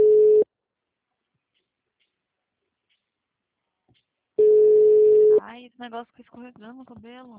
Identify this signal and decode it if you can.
Ringback tone (Europe, 425 Hz), with speech.